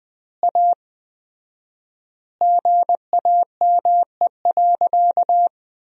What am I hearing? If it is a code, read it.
Morse code, heard as a keyed tone, about 20 wpm, 700 Hz. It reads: A GAME.
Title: 4x Word-Spacing - Sentences from Top 500 Words - 20wpm